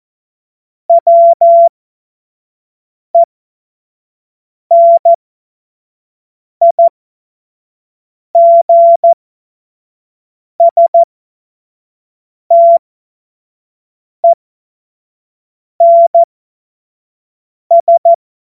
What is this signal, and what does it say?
Morse code, heard as a keyed tone, about 14 wpm, spaced as for 5 wpm, 680 Hz. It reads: WENIGSTENS
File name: Morsekurs / Witze / Witz 29 April 2025 25 F70